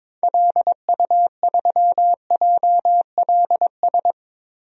Morse code, heard as a keyed tone, about 21 wpm, 700 Hz. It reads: LU3JLS
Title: DXCC-100 Call Signs - 6 Characters - Repeated 3x - 22wpm